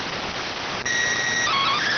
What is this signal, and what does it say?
SSTV (slow-scan television), PD 290.